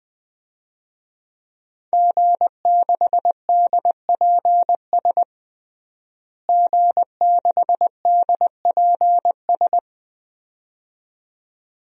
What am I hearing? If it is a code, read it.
Morse code, heard as a keyed tone, about 20 wpm, 700 Hz. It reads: G6DPS G6DPS
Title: DXCC-100 Call Signs - 5 Characters - Repeated 2x - 20wpm